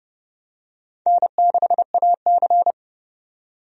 Morse code, heard as a keyed tone, about 30 wpm, 700 Hz. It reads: N6AC
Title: USA Call Signs - 4 Characters - 30wpm